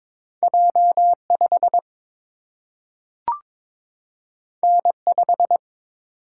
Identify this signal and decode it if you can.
Morse code, heard as a keyed tone, about 22 wpm, 700 Hz. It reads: J 5 N 5
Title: Single-Letter Single-Number 22wpm